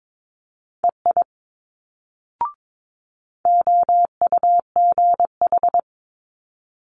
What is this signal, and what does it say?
Morse code, heard as a keyed tone, about 22 wpm, 700 Hz. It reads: EI OUGH
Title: Multi-letter Phonemes Quick Fire 22wpm